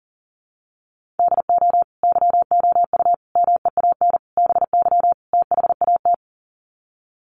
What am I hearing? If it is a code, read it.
Morse code, heard as a keyed tone, about 40 wpm, 700 Hz. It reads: DO YOU MEAN BY THAT
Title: Sets of 5 Words Encore2 40wpm